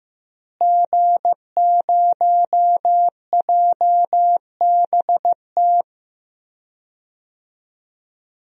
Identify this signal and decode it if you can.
Morse code, heard as a keyed tone, about 14 wpm, 700 Hz. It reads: G0JBT
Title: DXCC-100 Call Signs - 5 Characters - Repeated 3x - 15wpm